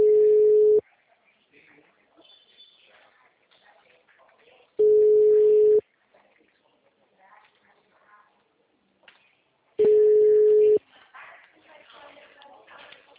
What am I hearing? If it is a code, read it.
Ringback tone (Europe, 425 Hz), with speech.
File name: out- -2006- - - .5273